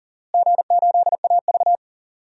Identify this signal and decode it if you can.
Morse code, heard as a keyed tone, about 40 wpm, 700 Hz. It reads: G8AV
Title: DXCC-100 Call Signs - 4 Characters - Repeated 2x - 40wpm